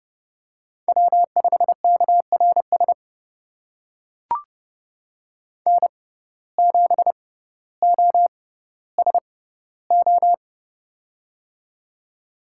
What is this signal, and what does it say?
Morse code, heard as a keyed tone, about 32 wpm, spaced as for 16 wpm, 700 Hz. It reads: W5KRS N7OSO
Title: USA Call Signs - 5 Characters - 6x Character Spacing - 30wpm